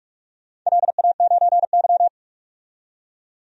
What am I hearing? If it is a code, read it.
Morse code, heard as a keyed tone, about 44 wpm, 700 Hz. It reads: LA9Y
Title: DXCC-100 Call Signs - 4 Characters - 45wpm